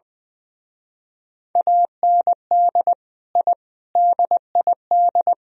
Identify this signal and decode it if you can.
Morse code, heard as a keyed tone, about 20 wpm, 700 Hz. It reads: AND I DID
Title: Speed-Racing - Sets of 5 Words 20wpm